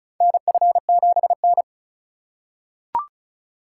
Morse code, heard as a keyed tone, about 36 wpm, 700 Hz. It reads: NF7N
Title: USA Call Signs - 4 Characters - 2x Character Spacing - 35wpm